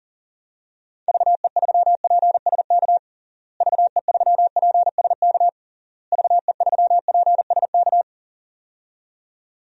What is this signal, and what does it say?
Morse code, heard as a keyed tone, about 40 wpm, 700 Hz. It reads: VE3PSK VE3PSK VE3PSK
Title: DXCC-100 Call Signs - 6 Characters - Repeated 3x - 40wpm